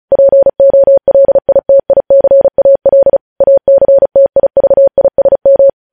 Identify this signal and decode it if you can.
Morse code, heard as a keyed tone, about 36 wpm, 550 Hz. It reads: POLITICAL ACTIVISM
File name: podcast / News headlines-1x-35x35